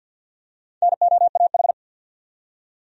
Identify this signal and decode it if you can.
Morse code, heard as a keyed tone, about 51 wpm, 700 Hz. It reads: NOAH
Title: USA Common Names - Max 5 Letters - 50wpm